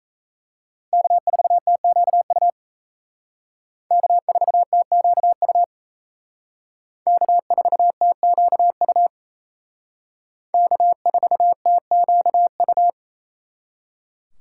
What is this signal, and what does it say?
Morse code, heard as a keyed tone, about 34 wpm, 700 Hz. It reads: K4TQU K4TQU K4TQU K4TQU
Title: Speed Racing - USA 1x3 Call Signs 28wpm